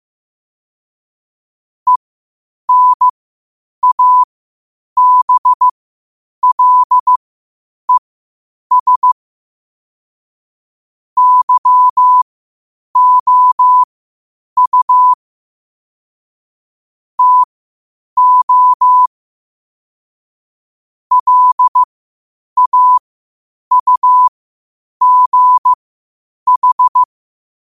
Morse code, heard as a keyed tone, about 15 wpm, spaced as for 8 wpm, 1000 Hz. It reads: ENABLES YOU TO LAUGH
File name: qotdfiles / 8wpm